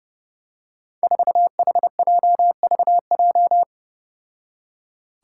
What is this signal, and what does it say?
Morse code, heard as a keyed tone, about 30 wpm, 700 Hz. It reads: 4HJVJ
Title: Five Challenging Letter-Number Combinations 30wpm